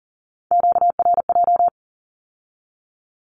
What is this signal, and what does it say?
Morse code, heard as a keyed tone, about 40 wpm, 700 Hz. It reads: QRJ
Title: Q-Codes with Spoken Meaning - 40wpm